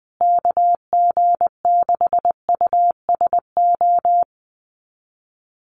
Morse code, heard as a keyed tone, about 20 wpm, 700 Hz. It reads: KG6USO
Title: USA Call Signs Repeated 3x 20wpm